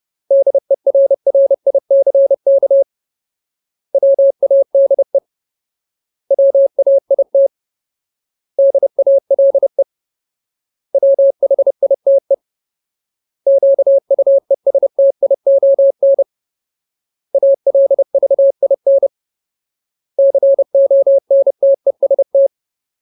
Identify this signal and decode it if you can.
Morse code, heard as a keyed tone, about 30 wpm, 550 Hz. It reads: DERRICK WADE WAIT DALE WHITE QUESTION ALVIN CONTEST
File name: Ultimate Word List / 30WPM Ultimate